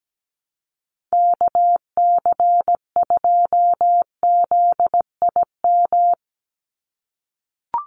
Morse code, heard as a keyed tone, about 17 wpm, 700 Hz. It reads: KC2ZIM
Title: USA Call Signs 17wpm